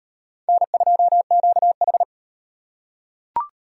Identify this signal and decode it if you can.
Morse code, heard as a keyed tone, about 37 wpm, 700 Hz. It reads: N2QH